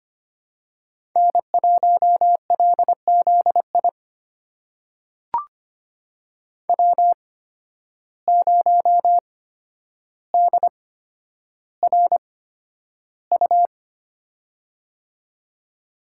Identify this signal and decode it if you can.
Morse code, heard as a keyed tone, about 25 wpm, spaced as for 11 wpm, 700 Hz. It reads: N1LZI W0DRU
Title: USA Call Signs - 8x Character Spacing - 25wpm